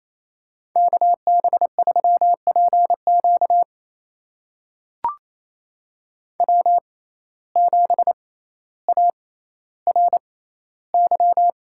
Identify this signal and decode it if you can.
Morse code, heard as a keyed tone, about 28 wpm, spaced as for 14 wpm, 700 Hz. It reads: KB3PQ W7ARY
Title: USA Call Signs - 5 Characters - 6x Character Spacing - 28wpm